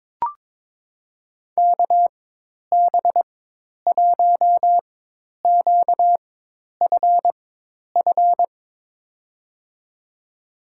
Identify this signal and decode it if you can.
Morse code, heard as a keyed tone, about 22 wpm, spaced as for 10 wpm, 700 Hz. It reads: KB1QFF
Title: USA Call Signs - 6 Characters - 4x Character Spacing - 22wpm